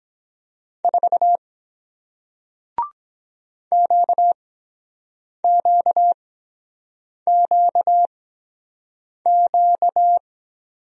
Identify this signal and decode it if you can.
Morse code, heard as a keyed tone, about 22 wpm, spaced as for 7 wpm, 700 Hz. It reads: 4 QQQQ